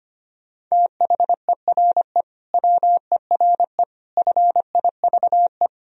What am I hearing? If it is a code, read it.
Morse code, heard as a keyed tone, about 25 wpm, 700 Hz. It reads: THERE WERE FIVE
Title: Sentences from Top 1000 Words - Review Part 3 - 25wpm